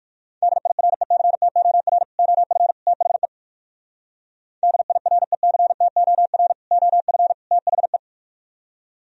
Morse code, heard as a keyed tone, about 58 wpm, 700 Hz. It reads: DIRECTOR OF THE DIRECTOR OF THE